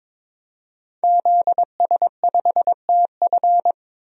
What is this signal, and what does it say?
Morse code, heard as a keyed tone, about 22 wpm, 700 Hz. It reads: ZS5TF
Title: DXCC-100 Call Signs - 5 Characters - 22wpm